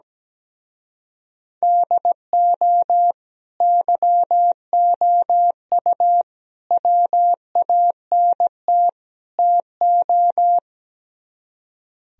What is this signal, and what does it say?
Morse code, heard as a keyed tone, about 18 wpm, 700 Hz. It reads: DO YOU WANT TO
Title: Speed-Racing - Sets of 4 Words 17wpm